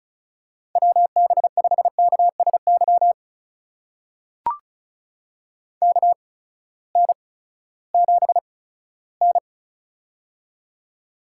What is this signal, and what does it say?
Morse code, heard as a keyed tone, about 36 wpm, spaced as for 18 wpm, 700 Hz. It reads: WB5KSY KN7N
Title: USA Call Signs - 8x Character Spacing - 35wpm